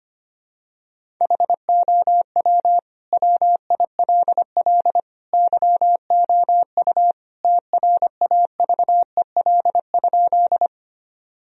Morse code, heard as a keyed tone, about 25 wpm, 700 Hz. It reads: HOW WILL YOU TRAVEL?
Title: Sentences from Top 1000 Words - Review Part 2 - 25wpm